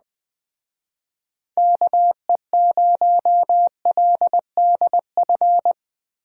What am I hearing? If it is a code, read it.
Morse code, heard as a keyed tone, about 20 wpm, 700 Hz. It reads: KE0LDF